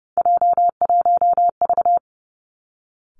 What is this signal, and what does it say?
Morse code, heard as a keyed tone, about 30 wpm, 700 Hz. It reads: J1V
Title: Three Challenging Letter-Number Combinations 30wpm